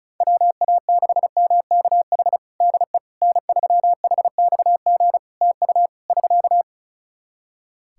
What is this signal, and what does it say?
Morse code, heard as a keyed tone, about 34 wpm, 700 Hz. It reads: WA6MKH DE N3HXG TU <SK>
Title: Practice QSO - Exchange 4 - Ending - 35wpm